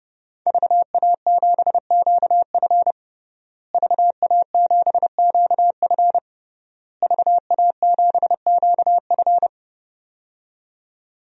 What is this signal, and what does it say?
Morse code, heard as a keyed tone, about 32 wpm, 700 Hz. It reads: VA7QF VA7QF VA7QF
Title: DXCC-100 Call Signs - 5 Characters - Repeated 3x - 30wpm